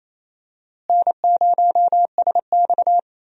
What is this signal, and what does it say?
Morse code, heard as a keyed tone, about 28 wpm, 700 Hz. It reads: N0SX
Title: USA Call Signs - 4 Characters - Repeated 3x - 28wpm